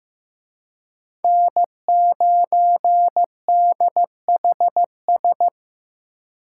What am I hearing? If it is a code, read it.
Morse code, heard as a keyed tone, about 18 wpm, 700 Hz. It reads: N9DHS
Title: Speed Racing - USA Call Signs 15wpm